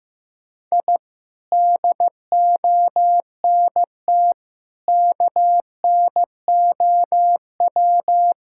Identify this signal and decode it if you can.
Morse code, heard as a keyed tone, about 15 wpm, 700 Hz. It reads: I DONT KNOW
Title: Sentences from Top 1000 Words - Review Part 5 - 15wpm